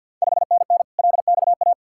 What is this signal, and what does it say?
Morse code, heard as a keyed tone, about 51 wpm, 700 Hz. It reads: 5NN LXA
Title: Ontario QSO Party Exchange 50wpm